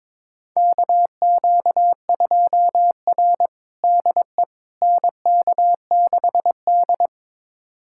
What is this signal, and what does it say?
Morse code, heard as a keyed tone, about 22 wpm, 700 Hz. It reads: KQ2R DE NK6D
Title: USA Callsign DE Callsign - 4 Characters - 22wpm